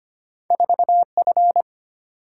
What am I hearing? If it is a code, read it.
Morse code, heard as a keyed tone, about 25 wpm, 690 Hz. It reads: 4F
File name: Two Letter-Number Combinations - 25wpm